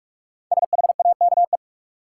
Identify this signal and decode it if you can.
Morse code, heard as a keyed tone, about 43 wpm, 700 Hz. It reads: SHAKE
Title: Five Letter Words - 45wpm